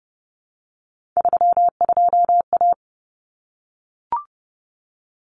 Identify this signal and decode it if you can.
Morse code, heard as a keyed tone, about 30 wpm, 700 Hz. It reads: 32A
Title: Common Cut Numbers - 3-Digit 30wpm